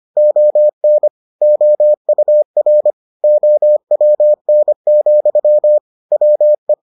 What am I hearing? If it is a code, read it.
Morse code, heard as a keyed tone, about 25 wpm, 600 Hz. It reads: ON OUR OWN, WE